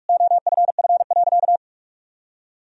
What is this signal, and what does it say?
Morse code, heard as a keyed tone, about 46 wpm, 700 Hz. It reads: OFF.
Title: Sentences from Top 400 Words and Abbreviations 45wpm